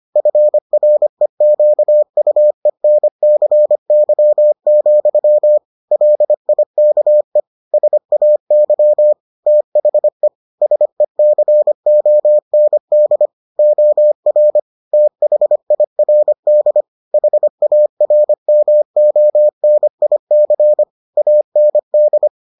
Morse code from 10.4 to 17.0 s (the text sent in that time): SECOND OR THIRD